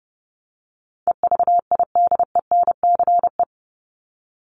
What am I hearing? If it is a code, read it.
Morse code, heard as a keyed tone, about 30 wpm, 700 Hz. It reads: EVIDENCE
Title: Top 1001-1250 Words 30wpm